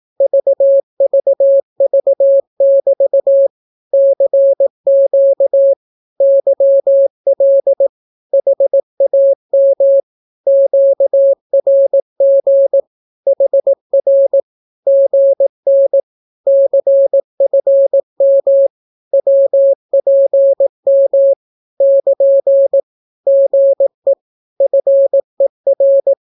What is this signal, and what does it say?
Morse code, heard as a keyed tone, about 18 wpm, 550 Hz. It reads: VVV<BT> CQ YL HAM QRG HR GN CFM WPM <KN> GE FER
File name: naukapodstaw / skroty / skroty